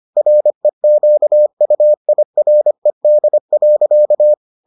Morse code, heard as a keyed tone, about 25 wpm, 600 Hz. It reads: REQUIRED.